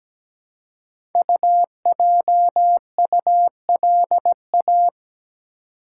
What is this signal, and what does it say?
Morse code, heard as a keyed tone, about 17 wpm, 700 Hz. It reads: UJULA
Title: Code Groups - 5 Letters - 17wpm